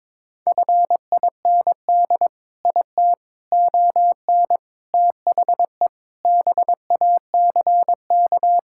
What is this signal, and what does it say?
Morse code, heard as a keyed tone, about 22 wpm, 700 Hz. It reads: FIND IT ON THE BACK
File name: practice / 22wpm / Sentences with 4-Word Phrases - 22wpm